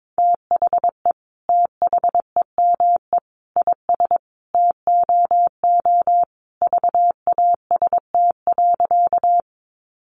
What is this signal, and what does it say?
Morse code, heard as a keyed tone, about 22 wpm, 700 Hz. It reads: THE THEME IS TOO VAST.
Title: Sentences from Top 1300 Words 22wpm